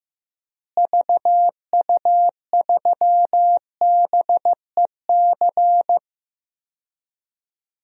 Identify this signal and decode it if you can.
Morse code, heard as a keyed tone, about 15 wpm, 700 Hz. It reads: VU3BEC
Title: DXCC-100 Call Signs - 6 Characters - Repeated 2x - 15wpm